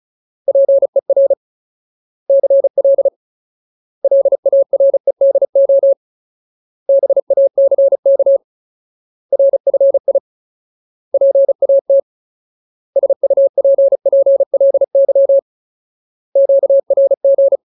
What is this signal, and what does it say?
Morse code, heard as a keyed tone, about 34 wpm, 550 Hz. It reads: PER CL LAREDO BACK RFI PAT SUPPLY QRG
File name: Ultimate Word List / 35WPM Ultimate